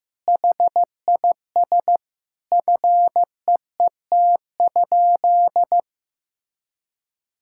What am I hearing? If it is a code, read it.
Morse code, heard as a keyed tone, about 15 wpm, 700 Hz. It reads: HIS FEET?
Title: Sentences from Top 1000 Words - Review Part 6 - 15wpm